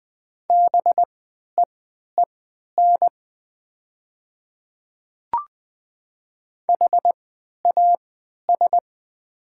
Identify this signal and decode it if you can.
Morse code, heard as a keyed tone, about 20 wpm, spaced as for 11 wpm, 700 Hz. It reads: BEEN HAS
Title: 3x Character Spacing - Top 200 Words - 20wpm